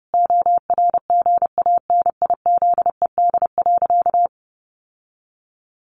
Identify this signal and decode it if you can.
Morse code, heard as a keyed tone, about 30 wpm, 700 Hz. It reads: ORGANIZED.
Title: Sentences from Top 1300 Words 30wpm